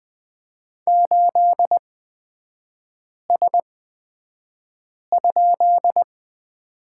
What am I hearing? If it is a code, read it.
Morse code, heard as a keyed tone, about 20 wpm, spaced as for 5 wpm, 700 Hz. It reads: 8S?
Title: 243 Lesson 39e - Letter z - Three Characters 20wpm